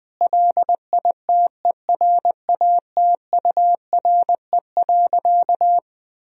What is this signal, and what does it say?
Morse code, heard as a keyed tone, about 20 wpm, 700 Hz. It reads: LITERATURE.
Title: Sentences from Top 1000 Words - Review Part 3 - 20wpm